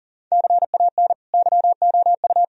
Morse code, heard as a keyed tone, about 37 wpm, 700 Hz. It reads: CAN YOU
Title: Sentences from Top 1000 Words - Review Part 4 - 40wpm